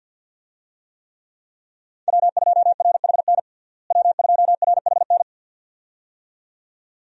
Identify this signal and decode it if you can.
Morse code, heard as a keyed tone, about 51 wpm, 700 Hz. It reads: W2RHN W2RHN
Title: USA Call Signs - 5 Characters - Repeated 2x - 50wpm